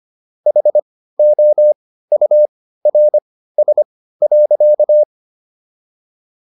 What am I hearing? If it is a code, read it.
Morse code, heard as a keyed tone, about 25 wpm, spaced as for 15 wpm, 600 Hz. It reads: HOURS.